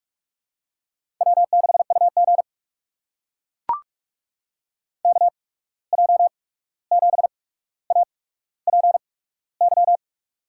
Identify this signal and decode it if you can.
Morse code, heard as a keyed tone, about 42 wpm, spaced as for 17 wpm, 700 Hz. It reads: W6UG KJ7APY
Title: USA Call Signs - 8x Character Spacing - 45wpm